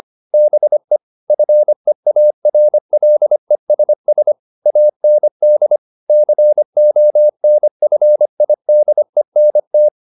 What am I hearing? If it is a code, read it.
Morse code, heard as a keyed tone, about 25 wpm, 600 Hz. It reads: BE FEARLESS AND CONFIDENT